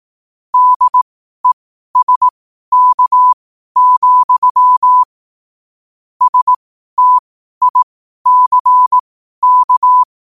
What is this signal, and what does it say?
Morse code, heard as a keyed tone, about 18 wpm, spaced as for 12 wpm, 1000 Hz. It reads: DESK, STICK
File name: qotdfiles / 12wpm